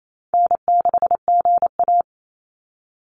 Morse code, heard as a keyed tone, about 28 wpm, 700 Hz. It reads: N6GA
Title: Easy Call Signs - 28wpm